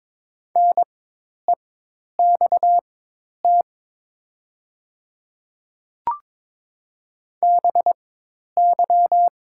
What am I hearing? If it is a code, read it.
Morse code, heard as a keyed tone, about 22 wpm, spaced as for 10 wpm, 700 Hz. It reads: NEXT BY
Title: 4x Character Spacing - Top 300 Words - 22wpm